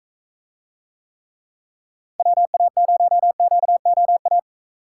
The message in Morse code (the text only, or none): WA0QOA